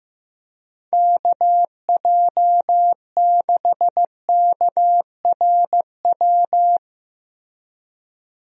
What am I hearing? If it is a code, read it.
Morse code, heard as a keyed tone, about 15 wpm, 700 Hz. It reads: KJ6KRW
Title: USA Call Signs - 6 Characters - Repeated 3x - 15wpm